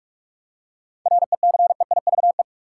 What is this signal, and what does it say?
Morse code, heard as a keyed tone, about 45 wpm, 700 Hz. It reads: RECEIVE
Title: Seven Letter Words - 45wpm